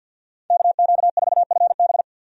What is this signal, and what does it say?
Morse code, heard as a keyed tone, about 52 wpm, 700 Hz. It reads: KQ4FB
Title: USA Call Signs - 5 Characters - 50wpm